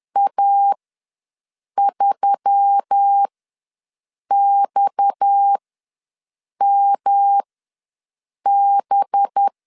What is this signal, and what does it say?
Morse code, heard as a keyed tone, about 10 wpm, spaced as for 6 wpm, 800 Hz. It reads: A3XMB
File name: LEZIONI / 6 PKX0 / 6 PKX ws 10wpm 3706